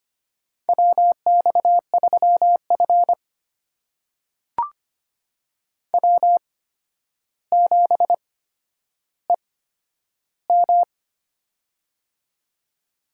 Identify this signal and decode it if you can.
Morse code, heard as a keyed tone, about 25 wpm, spaced as for 11 wpm, 700 Hz. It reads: WX3F W7EM
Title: USA Call Signs - 8x Character Spacing - 25wpm